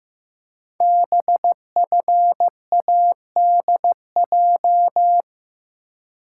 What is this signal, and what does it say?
Morse code, heard as a keyed tone, about 15 wpm, 700 Hz. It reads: BFADJ
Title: Code Groups - 5 Letters - 15wpm